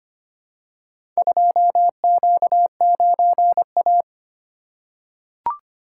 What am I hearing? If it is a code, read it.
Morse code, heard as a keyed tone, about 25 wpm, 700 Hz. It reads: 2Q9A
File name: practice / 25wpm / Four Letter-Number Combinations - 25wpm